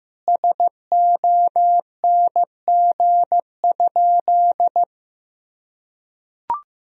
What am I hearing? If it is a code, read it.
Morse code, heard as a keyed tone, about 15 wpm, 700 Hz. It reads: SONG?
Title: Sentences from Top 1000 Words - Review Part 2 - 15wpm